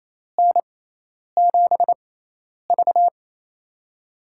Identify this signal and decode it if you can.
Morse code, heard as a keyed tone, about 28 wpm, spaced as for 10 wpm, 700 Hz. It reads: N7V
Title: USA Call Signs - 4 Characters - 6x Character Spacing - 28wpm